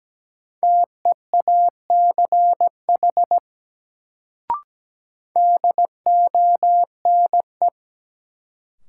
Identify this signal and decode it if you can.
Morse code, heard as a keyed tone, about 17 wpm, 700 Hz. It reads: TEACH DONE
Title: Sentences from Top 500 Words - New Word Drill - 17wpm